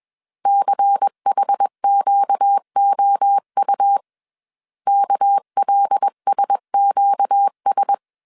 Morse code, heard as a keyed tone, about 21 wpm, 800 Hz. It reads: CHQOU KLSQS
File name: CW / DET / 1 6 1500C S A 20wpm 2048